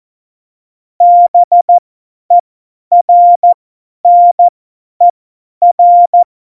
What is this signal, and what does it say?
Morse code, heard as a keyed tone, about 14 wpm, spaced as for 10 wpm, 700 Hz. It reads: BERNER
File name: Morsekurs / Witze / Witz 17. November 2020, 50 F70